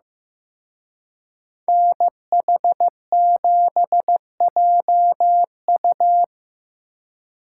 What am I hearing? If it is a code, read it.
Morse code, heard as a keyed tone, about 18 wpm, 700 Hz. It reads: NH7JU
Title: Speed Racing - USA Call Signs 15wpm